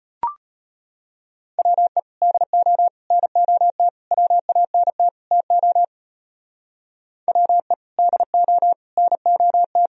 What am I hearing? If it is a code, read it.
Morse code, heard as a keyed tone, about 30 wpm, 700 Hz. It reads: WE DO NOT WANT TO WE DO NOT
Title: Speed-Racing - Sets of 5 Words 25wpm